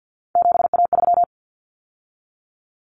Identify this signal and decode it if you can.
Morse code, heard as a keyed tone, about 50 wpm, 700 Hz. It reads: 7A3
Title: Common Cut Numbers - 3-Digit 50wpm